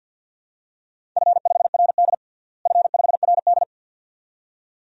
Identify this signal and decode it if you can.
Morse code, heard as a keyed tone, about 50 wpm, 700 Hz. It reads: F5RD F5RD
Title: DXCC-100 Call Signs - 4 Characters - Repeated 2x - 50wpm